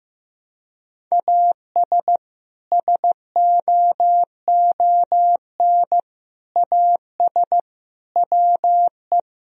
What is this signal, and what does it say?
Morse code, heard as a keyed tone, about 15 wpm, 700 Hz. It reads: AS SOON AS WE
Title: Sentences from Top 1000 Words - Review Part 2 - 15wpm